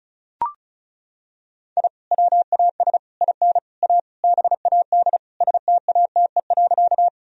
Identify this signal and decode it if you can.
Morse code, heard as a keyed tone, about 36 wpm, 700 Hz. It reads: I WAS IN A BAD STATE.
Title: Sentences from Top 1000 Words - Review Part 2 - 35wpm